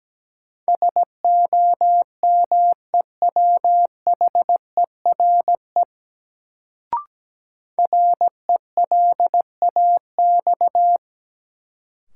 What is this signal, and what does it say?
Morse code, heard as a keyed tone, about 17 wpm, 700 Hz. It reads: SOMEWHERE RELAX
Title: Sentences from Top 1400 Words - New Word Drill - 17wpm